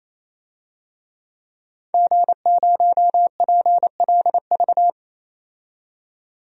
Morse code, heard as a keyed tone, about 28 wpm, 700 Hz. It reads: G0PLV
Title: DXCC-100 Call Signs - 5 Characters - 28wpm